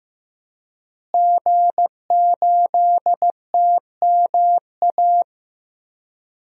Morse code, heard as a keyed tone, about 15 wpm, 700 Hz. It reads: G8TMA
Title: DXCC-100 Call Signs - 5 Characters - Repeated 2x - 15wpm